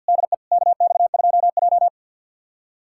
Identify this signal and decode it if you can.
Morse code, heard as a keyed tone, about 53 wpm, 700 Hz. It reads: DE KX2J
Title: USA Callsign DE Callsign - 4 Characters - 50wpm